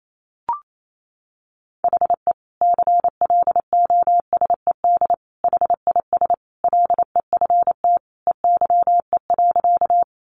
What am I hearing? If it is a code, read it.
Morse code, heard as a keyed tone, about 28 wpm, 700 Hz. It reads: HE CLOSED HIS LEFT EYE.
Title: Sentences from Top 1400 Words 28wpm